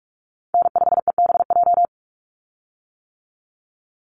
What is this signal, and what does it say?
Morse code, heard as a keyed tone, about 44 wpm, 700 Hz. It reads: N5EBJ
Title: USA Call Signs 45wpm